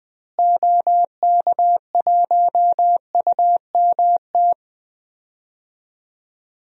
Morse code, heard as a keyed tone, about 20 wpm, 700 Hz. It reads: OK1UMT
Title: DXCC-100 Call Signs - 6 Characters - Repeated 3x - 20wpm